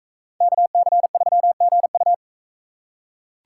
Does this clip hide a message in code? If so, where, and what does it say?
Yes: Morse code from 0.4 to 2.1 s, heard as a keyed tone, about 43 wpm, 700 Hz. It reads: KC3GU